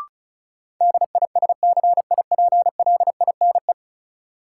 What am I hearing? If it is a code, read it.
Morse code, heard as a keyed tone, about 36 wpm, 700 Hz. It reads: DISCIPLINE
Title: Sentences from Top 900 Words - New Word Drill - 35wpm